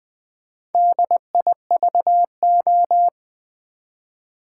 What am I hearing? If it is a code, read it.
Morse code, heard as a keyed tone, about 20 wpm, 700 Hz. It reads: DIVO
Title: Build A Word - 2 Letters At A Time - 8-Letter Words - 20wpm